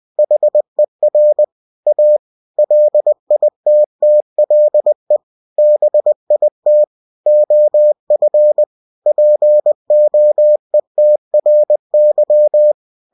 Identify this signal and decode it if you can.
Morse code, heard as a keyed tone, about 20 wpm, 600 Hz. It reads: HER A LITTLE BIT OF POETRY
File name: cw-books / winnie-the-pooh / winnie-20wpm-page-0065